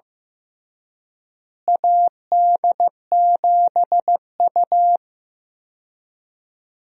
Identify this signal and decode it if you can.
Morse code, heard as a keyed tone, about 15 wpm, 700 Hz. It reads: AD7U
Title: USA Call Signs - 4 Characters - Repeated 3x - 15wpm